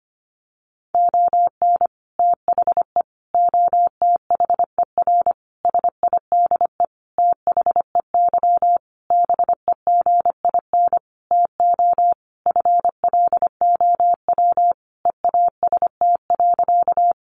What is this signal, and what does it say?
Morse code, heard as a keyed tone, about 25 wpm, 700 Hz. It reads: ON THE OTHER SIDE THEY BEGIN TO FLOW EAST.
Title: Sentences from Top 1200 Words 25wpm